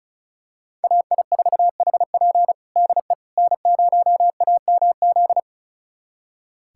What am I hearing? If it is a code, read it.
Morse code, heard as a keyed tone, about 34 wpm, 700 Hz. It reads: AI4HP DE N0AMZ
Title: USA Callsign DE Callsign - 5 Characters - 35wpm